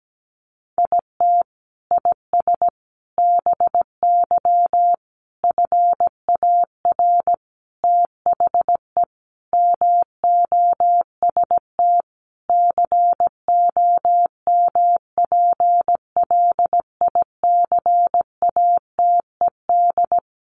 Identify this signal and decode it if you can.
Morse code, heard as a keyed tone, about 17 wpm, 700 Hz. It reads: IT IS BY FAR THE MOST COMPLICATED
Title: Sentences from Top 1200 Words 17wpm